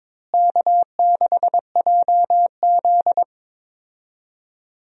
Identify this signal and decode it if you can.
Morse code, heard as a keyed tone, about 22 wpm, 700 Hz. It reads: K6JZ
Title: USA Call Signs - 4 Characters - Repeated 2x - 22wpm